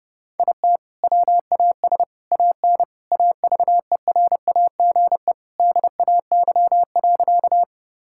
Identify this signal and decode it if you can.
Morse code, heard as a keyed tone, about 32 wpm, 700 Hz. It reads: IT WAS AN AVERAGE DAY.
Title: Sentences from Top 1000 Words - Review Part 6 - 30wpm